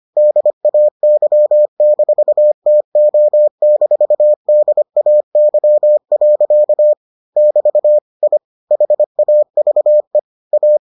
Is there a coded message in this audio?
Yes: Morse code from 0.2 to 10.8 s, heard as a keyed tone, about 25 wpm, 600 Hz. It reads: DAY-TO-DAY. <BT> I HAVE A